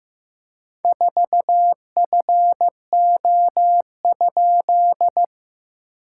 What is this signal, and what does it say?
Morse code, heard as a keyed tone, about 15 wpm, 700 Hz. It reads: 4FO?
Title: SOTA QSO Practice 15wpm